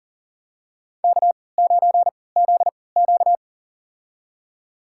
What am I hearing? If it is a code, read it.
Morse code, heard as a keyed tone, about 40 wpm, spaced as for 23 wpm, 700 Hz. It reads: K9ZQ
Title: USA Call Signs - 3x Character Spacing - 40wpm